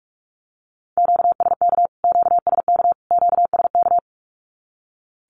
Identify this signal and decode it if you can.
Morse code, heard as a keyed tone, about 44 wpm, 700 Hz. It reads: QSK QSK QSK
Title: Q-Codes Repeated 3x 45wpm